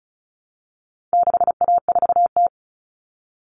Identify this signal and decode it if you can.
Morse code, heard as a keyed tone, about 34 wpm, 700 Hz. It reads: 6A4T
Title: Common Cut Numbers - 4-Digit 35wpm